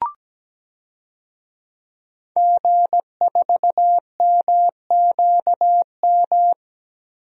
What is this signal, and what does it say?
Morse code, heard as a keyed tone, about 17 wpm, 700 Hz. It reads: G4MQM
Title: DXCC-100 Call Signs - 5 Characters - Repeated 3x - 17wpm